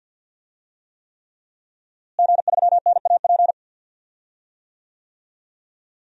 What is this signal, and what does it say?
Morse code, heard as a keyed tone, about 48 wpm, 700 Hz. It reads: G3NAP
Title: DXCC-100 Call Signs - 5 Characters - Repeated 3x - 50wpm